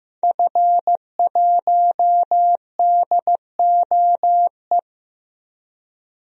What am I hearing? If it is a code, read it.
Morse code, heard as a keyed tone, about 16 wpm, 700 Hz. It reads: F1DOE
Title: DXCC-100 Call Signs - 5 Characters - Repeated 2x - 15wpm